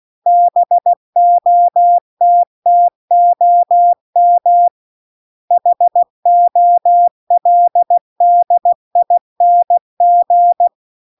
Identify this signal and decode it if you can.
Morse code, heard as a keyed tone, about 16 wpm, 700 Hz. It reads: BOTTOM HOLDING